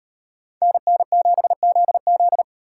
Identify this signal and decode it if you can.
Morse code, heard as a keyed tone, about 38 wpm, 700 Hz. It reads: NN7ZZ